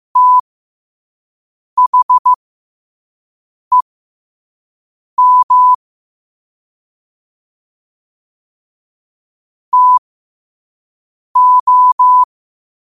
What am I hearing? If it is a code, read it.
Morse code, heard as a keyed tone, about 15 wpm, spaced as for 5 wpm, 1000 Hz. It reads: THEM TO
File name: qotdfiles / 5wpm